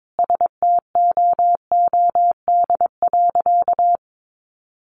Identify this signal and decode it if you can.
Morse code, heard as a keyed tone, about 22 wpm, 700 Hz. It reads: STOOD.
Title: Sentences from Top 500 Words 22wpm